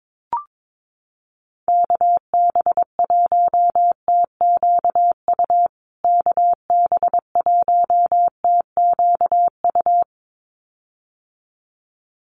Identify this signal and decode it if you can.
Morse code, heard as a keyed tone, about 22 wpm, 700 Hz. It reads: KB1TQU KB1TQU
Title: USA Call Signs Repeated 2x 22wpm